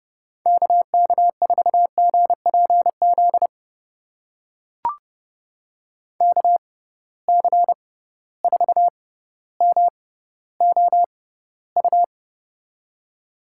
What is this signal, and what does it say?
Morse code, heard as a keyed tone, about 31 wpm, spaced as for 15 wpm, 700 Hz. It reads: KK4GPZ KC4MOU
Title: USA Call Signs - 6 Characters - 6x Character Spacing - 30wpm